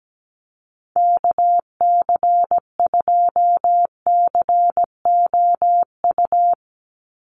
Morse code, heard as a keyed tone, about 17 wpm, 700 Hz. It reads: KC2COU
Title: USA Call Signs 17wpm